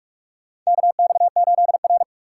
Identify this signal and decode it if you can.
Morse code, heard as a keyed tone, about 46 wpm, 700 Hz. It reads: KX8R
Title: USA Call Signs - 4 Characters - 3x Character Spacing - 45wpm